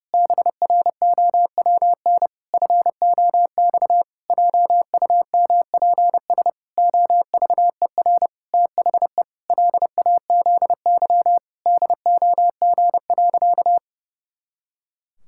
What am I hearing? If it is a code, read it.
Morse code, heard as a keyed tone, about 30 wpm, 700 Hz. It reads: BROWN FOX JUMPS OVER THE LAZY DOG.
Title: Pangram Challenge v2 - 30wpm